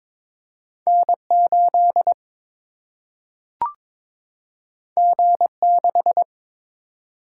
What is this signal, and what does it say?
Morse code, heard as a keyed tone, about 22 wpm, 700 Hz. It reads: N 8 G 6
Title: Single-Letter Single-Number 22wpm